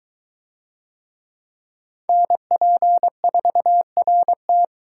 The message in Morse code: NP4RT

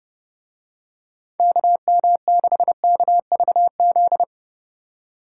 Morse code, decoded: KM6KVZ

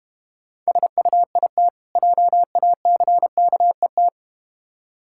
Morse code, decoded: SUIT JACKET

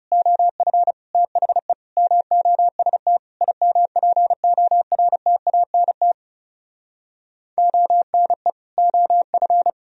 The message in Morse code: OF THE MOST IMPORTANT ONE OF

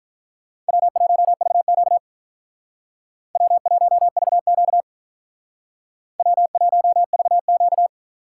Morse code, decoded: W1VQ W1VQ W1VQ

